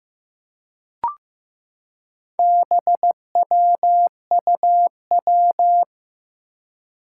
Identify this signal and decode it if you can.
Morse code, heard as a keyed tone, about 15 wpm, 700 Hz. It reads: BWUW
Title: Four Challenging Letter-Number Combinations 15wpm